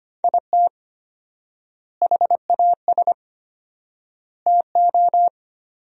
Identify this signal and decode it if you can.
Morse code, heard as a keyed tone, about 25 wpm, 700 Hz. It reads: IT HAS TO
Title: 4x Word-Spacing - Sentences from Top 500 Words - 25wpm